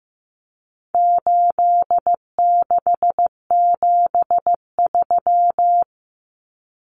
Morse code, heard as a keyed tone, about 15 wpm, 700 Hz. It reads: 8673
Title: Common Cut Numbers - 4-Digit 15wpm